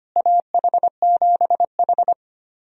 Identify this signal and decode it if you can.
Morse code, heard as a keyed tone, about 25 wpm, 700 Hz. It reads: AH7H
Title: USA Call Signs - 4 Characters - Repeated 3x - 25wpm